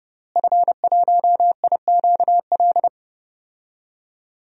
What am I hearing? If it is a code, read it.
Morse code, heard as a keyed tone, about 30 wpm, 700 Hz. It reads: F1IQL
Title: DXCC-100 Call Signs - 5 Characters - Repeated 2x - 30wpm